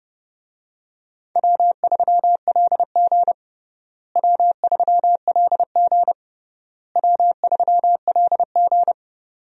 Morse code, heard as a keyed tone, about 31 wpm, 700 Hz. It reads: W3LG W3LG W3LG
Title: USA Call Signs - 4 Characters - Repeated 3x - 30wpm